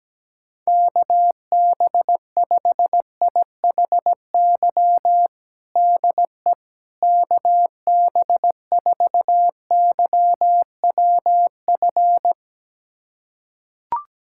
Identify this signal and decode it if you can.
Morse code, heard as a keyed tone, about 17 wpm, 700 Hz. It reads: KB5IHY DE KB4YWF
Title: USA Callsign DE Callsign - 6 Characters - 17wpm